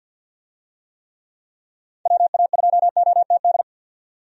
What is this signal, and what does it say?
Morse code, heard as a keyed tone, about 52 wpm, 700 Hz. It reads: WA2OTD
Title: USA Call Signs - 6 Characters - 50wpm